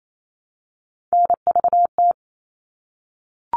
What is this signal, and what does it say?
Morse code, heard as a keyed tone, about 28 wpm, 700 Hz. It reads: NVT
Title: All Cut Numbers - 3-Digit 28wpm